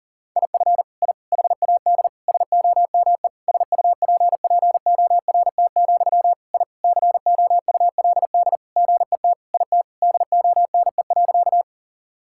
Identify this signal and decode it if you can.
Morse code, heard as a keyed tone, about 40 wpm, 700 Hz. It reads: IF I HAD SOME SUPPORT, I COULD GET IT DONE.
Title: Sentences from Top 1000 Words - Review Part 5 - 40wpm